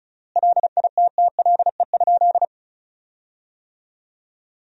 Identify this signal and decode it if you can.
Morse code, heard as a keyed tone, about 36 wpm, 700 Hz. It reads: LITTLE?
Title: 3x Word-Spacing - Sentences from Top 500 Words - 35wpm